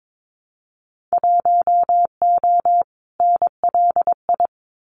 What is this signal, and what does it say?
Morse code, heard as a keyed tone, about 22 wpm, 700 Hz. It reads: 1O NLI
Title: Winter Field Day Exchanges 22wpm